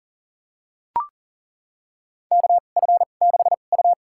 Morse code, heard as a keyed tone, about 40 wpm, spaced as for 29 wpm, 700 Hz. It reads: KF6U
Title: USA Call Signs - 4 Characters - 2x Character Spacing - 40wpm